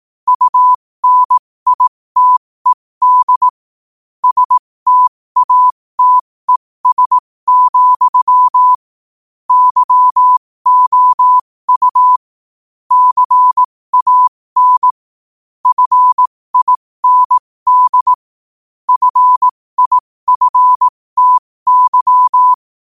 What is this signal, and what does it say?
Morse code, heard as a keyed tone, about 18 wpm, spaced as for 15 wpm, 1000 Hz. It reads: UNITED STATES, YOU CAN FIND FIFTY
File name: qotdfiles / 15wpm